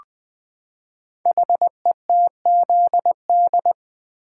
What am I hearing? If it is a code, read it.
Morse code, heard as a keyed tone, about 20 wpm, 700 Hz. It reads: HETZD
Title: Code Groups - 5 Letters - 20wpm